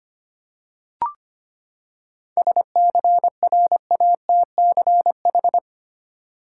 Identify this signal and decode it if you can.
Morse code, heard as a keyed tone, about 25 wpm, 700 Hz. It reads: SCRATCH
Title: Seven Letter Words - 25wpm